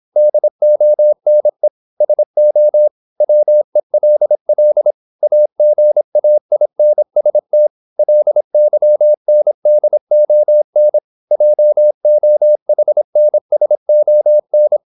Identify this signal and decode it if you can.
Morse code, heard as a keyed tone, about 26 wpm, 600 Hz. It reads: DONE SO WELL AGAINST LYNDON JOHNSON
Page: Never 26wpm | CW med Gnister